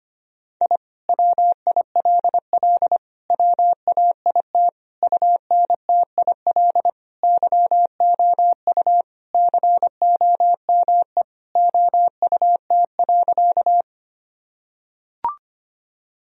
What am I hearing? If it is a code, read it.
Morse code, heard as a keyed tone, about 25 wpm, 700 Hz. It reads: I WILL WAIT UNTIL YOU COME OUT.
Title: Sentences from Top 1000 Words - Review Part 1 - 25wpm